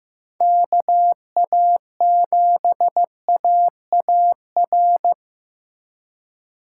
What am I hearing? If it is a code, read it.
Morse code, heard as a keyed tone, about 18 wpm, 700 Hz. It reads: KA7AAR
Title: Speed Racing - USA 2x3 Call Signs 15wpm